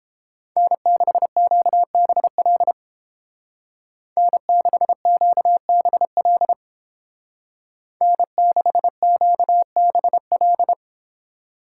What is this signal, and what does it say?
Morse code, heard as a keyed tone, about 31 wpm, 700 Hz. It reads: N6QBL N6QBL N6QBL